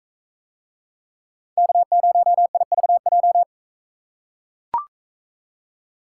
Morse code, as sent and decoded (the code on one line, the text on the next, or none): -.- ----- .. ...- .---
K0IVJ